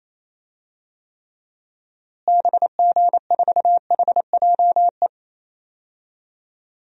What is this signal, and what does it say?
Morse code, heard as a keyed tone, about 28 wpm, 700 Hz. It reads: BG4HJE
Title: DXCC-100 Call Signs - 6 Characters - Repeated 3x - 28wpm